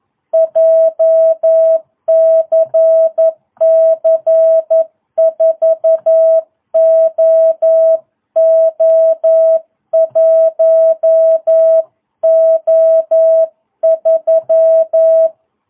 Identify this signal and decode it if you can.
Morse code, heard as a keyed tone, about 11 wpm, 660 Hz. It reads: JCC 4OO1O3